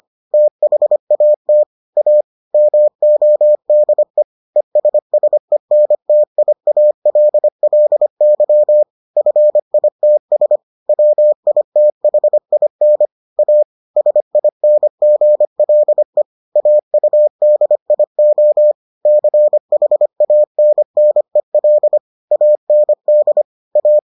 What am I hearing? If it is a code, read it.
Morse code, heard as a keyed tone, about 25 wpm, 600 Hz. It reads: THAT A MODE ESSENTIALLY FITS WITHIN A SINGLE AUDIO CHANNEL AND A